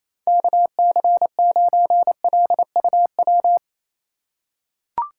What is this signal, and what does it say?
Morse code, heard as a keyed tone, about 28 wpm, 700 Hz. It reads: KC9LUW
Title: USA Call Signs - 6 Characters - 4x Character Spacing - 28wpm